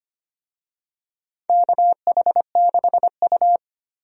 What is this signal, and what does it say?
Morse code, heard as a keyed tone, about 25 wpm, 700 Hz. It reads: KH6U
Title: DXCC-100 Call Signs - 4 Characters - 25wpm